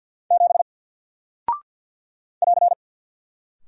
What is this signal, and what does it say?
Morse code, heard as a keyed tone, about 50 wpm, 700 Hz. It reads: 7 <AR>
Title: Single Character (Letter:Number:Punctuation:Prosign) 50wpm